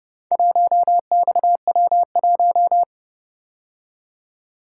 Morse code, heard as a keyed tone, about 29 wpm, 700 Hz. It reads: 1XW1